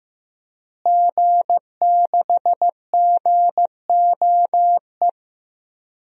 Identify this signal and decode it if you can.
Morse code, heard as a keyed tone, about 14 wpm, 700 Hz. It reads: G6GOE
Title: DXCC-100 Call Signs - 5 Characters - Repeated 2x - 15wpm